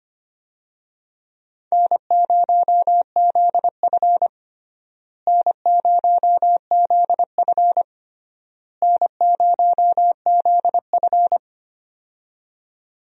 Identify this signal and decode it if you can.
Morse code, heard as a keyed tone, about 25 wpm, 700 Hz. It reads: N0ZF N0ZF N0ZF
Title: USA Call Signs - 4 Characters - Repeated 3x - 25wpm